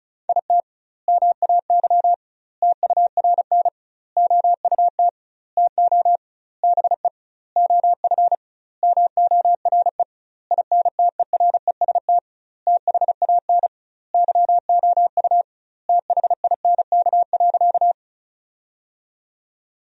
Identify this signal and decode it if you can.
Morse code, heard as a keyed tone, about 34 wpm, 700 Hz. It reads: IT MAY TURN OUT TO BE OF MORE INTEREST THAN YOU THINK.
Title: 2x Word-Spacing - Sentences from Top 400 Words - 35wpm